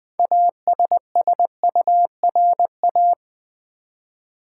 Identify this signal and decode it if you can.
Morse code, heard as a keyed tone, about 20 wpm, 700 Hz. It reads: ASSURA
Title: Build A Word - 3 Letters At A Time - 9-Letter Words - 20wpm